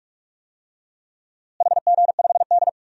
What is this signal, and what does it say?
Morse code, heard as a keyed tone, about 46 wpm, 700 Hz. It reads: HG5D
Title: DXCC-100 Call Signs - 4 Characters - Repeated 3x - 45wpm